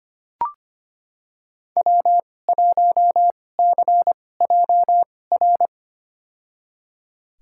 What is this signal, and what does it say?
Morse code, heard as a keyed tone, about 25 wpm, spaced as for 18 wpm, 700 Hz. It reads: W1CJR
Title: USA Call Signs - 5 Characters - 2x Character Spacing - 25wpm